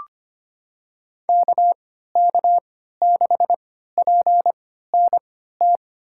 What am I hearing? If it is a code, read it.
Morse code, heard as a keyed tone, about 25 wpm, spaced as for 14 wpm, 700 Hz. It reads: KK6PNT
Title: USA Call Signs - 3x Character Spacing - 25wpm